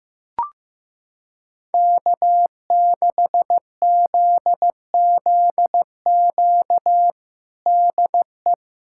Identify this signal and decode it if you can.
Morse code, heard as a keyed tone, about 15 wpm, 700 Hz. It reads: K6ZZQ DE
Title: USA Callsign DE Callsign - 5 Characters - 15wpm